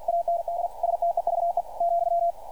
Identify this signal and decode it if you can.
Morse code, heard as a keyed tone, about 52 wpm, 690 Hz, with radio noise. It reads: AND STEVE ,
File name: archived / CRSnet